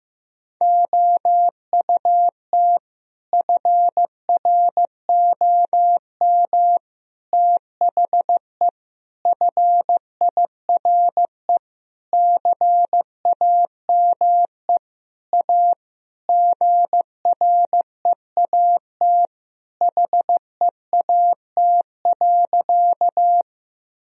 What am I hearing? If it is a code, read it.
Morse code, heard as a keyed tone, about 16 wpm, 700 Hz. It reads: OUT FROM THE FIRE CAME A GREAT HEAT.
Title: Sentences from Top 1000 Words - Review Part 3 - 15wpm